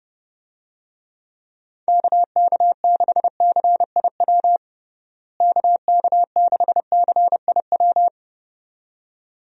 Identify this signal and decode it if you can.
Morse code, heard as a keyed tone, about 30 wpm, 700 Hz. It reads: KK6CIW KK6CIW
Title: USA Call Signs - 6 Characters - Repeated 2x - 30wpm